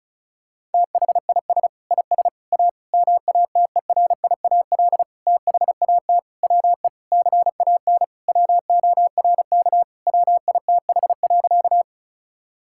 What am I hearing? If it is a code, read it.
Morse code, heard as a keyed tone, about 34 wpm, 700 Hz. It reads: THIS IS A MATERIAL THAT WE CAN WORK WITH.
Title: Sentences from Top 1000 Words - Review Part 5 - 35wpm